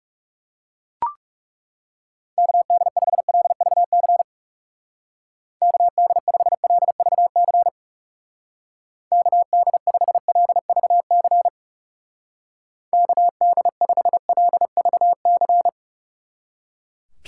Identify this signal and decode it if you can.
Morse code, heard as a keyed tone, about 37 wpm, 700 Hz. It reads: KD5LVC KD5LVC KD5LVC KD5LVC
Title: Speed Racing - USA 2x3 Call Signs 30wpm